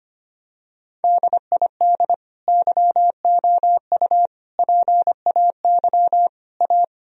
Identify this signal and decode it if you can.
Morse code, heard as a keyed tone, about 25 wpm, 700 Hz. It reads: DID YOU PAY A
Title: Sentences from Top 1000 Words - Review Part 4 - 25wpm